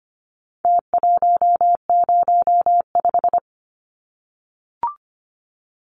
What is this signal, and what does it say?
Morse code, heard as a keyed tone, about 25 wpm, 700 Hz. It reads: T105
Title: Numbers - Contest Style - 4-Digit 25wpm